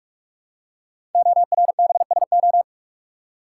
Morse code, heard as a keyed tone, about 44 wpm, 700 Hz. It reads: ORBSO
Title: Code Groups - 5 Letter:Numbers - 45wpm